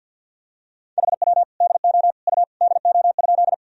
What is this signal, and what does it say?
Morse code, heard as a keyed tone, about 48 wpm, 700 Hz. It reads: HW DO U DO?
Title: Sentences from Top 100 Words and Abbreviations 50wpm